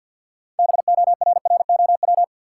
Morse code, heard as a keyed tone, about 53 wpm, 700 Hz. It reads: BORROW
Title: Sentences from Top 1500 Words - New Word Drill - 50wpm